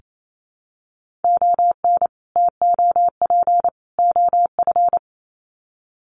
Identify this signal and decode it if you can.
Morse code, heard as a keyed tone, about 28 wpm, 700 Hz. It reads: ON TOP OF
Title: Sets of 3 Words 28wpm